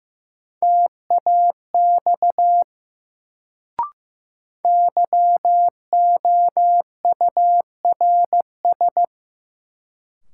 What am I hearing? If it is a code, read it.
Morse code, heard as a keyed tone, about 15 wpm, 700 Hz. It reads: TAX YOURS
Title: Sentences from Top 1500 Words - New Word Drill Alt - 15wpm